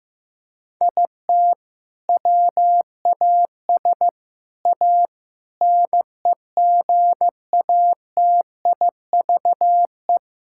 Morse code, heard as a keyed tone, about 15 wpm, 700 Hz. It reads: IT WAS A NEGATIVE
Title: Sentences from Top 1000 Words - Review Part 6 - 15wpm